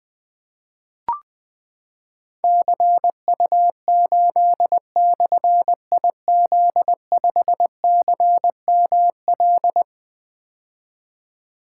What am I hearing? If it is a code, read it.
Morse code, heard as a keyed tone, about 20 wpm, 700 Hz. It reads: CU8/IZ5CML
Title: Difficult Call Signs 20wpm